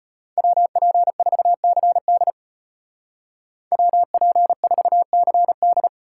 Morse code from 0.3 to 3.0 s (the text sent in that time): WP4CD